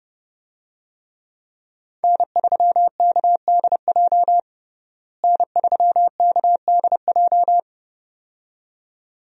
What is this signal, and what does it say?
Morse code, heard as a keyed tone, about 32 wpm, 700 Hz. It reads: N3KDJ N3KDJ
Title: USA Call Signs - 5 Characters - Repeated 2x - 30wpm